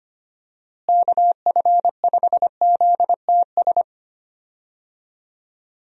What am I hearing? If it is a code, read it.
Morse code, heard as a keyed tone, about 25 wpm, 700 Hz. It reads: KF5ZTS
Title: USA Call Signs - 6 Characters - Repeated 3x - 25wpm